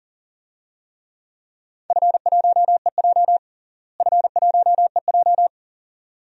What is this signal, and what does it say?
Morse code, heard as a keyed tone, about 40 wpm, 700 Hz. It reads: F1EJ F1EJ
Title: DXCC-100 Call Signs - 4 Characters - Repeated 2x - 40wpm